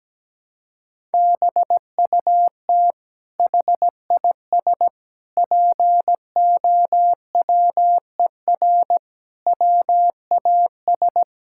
Morse code, heard as a keyed tone, about 17 wpm, 700 Hz. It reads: BUT HIS POWER WAS
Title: Sentences from Top 1000 Words - Review Part 6 - 17wpm